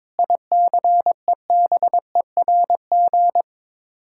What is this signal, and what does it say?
Morse code, heard as a keyed tone, about 22 wpm, 700 Hz. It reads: ICEBERG
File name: practice / 22wpm / Sets of 6 Words - 22wpm